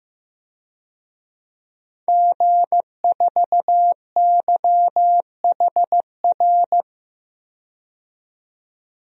Morse code, heard as a keyed tone, about 15 wpm, 700 Hz. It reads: G4YHR
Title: DXCC-100 Call Signs - 5 Characters - 15wpm